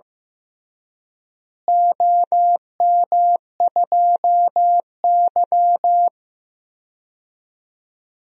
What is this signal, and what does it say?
Morse code, heard as a keyed tone, about 16 wpm, 700 Hz. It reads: OM2Y
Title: DXCC-100 Call Signs - 4 Characters - Repeated 3x - 15wpm